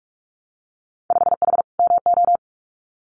Morse code, heard as a keyed tone, about 46 wpm, 700 Hz. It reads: 5H MO
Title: Winter Field Day Exchanges Encore 45wpm